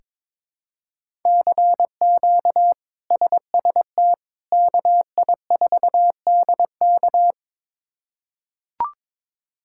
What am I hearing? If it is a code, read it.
Morse code, heard as a keyed tone, about 22 wpm, 700 Hz. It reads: CQ SST KI4DK
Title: K1USN SST QSO Practice - 22wpm